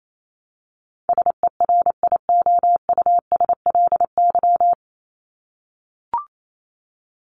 Morse code, heard as a keyed tone, about 28 wpm, 700 Hz. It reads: SERIOUSLY
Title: Top 1001-1250 Words 28wpm